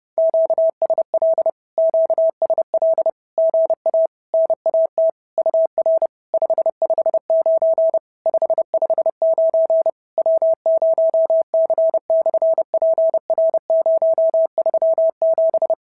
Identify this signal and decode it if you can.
Morse code, heard as a keyed tone, about 30 wpm, 650 Hz. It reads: QSL QSL GA NAT UR 559 559 W0C/PR037
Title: SOTA QSO Practice 30wpm